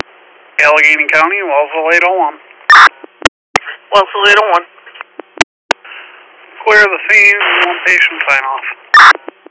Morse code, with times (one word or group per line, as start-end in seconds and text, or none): none